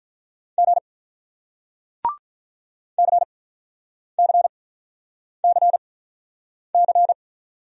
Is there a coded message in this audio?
Yes: Morse code from 0.6 to 7.1 s, heard as a keyed tone, about 47 wpm, spaced as for 9 wpm, 700 Hz. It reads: G CCCC